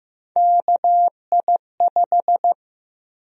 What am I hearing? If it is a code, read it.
Morse code, heard as a keyed tone, about 15 wpm, 700 Hz. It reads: KI5
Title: USA Call Sign Prefixes - 15wpm